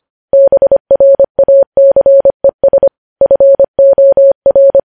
Morse code, heard as a keyed tone, about 25 wpm, 550 Hz. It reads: BRACES FOR
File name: podcast / News headlines-1x-25x25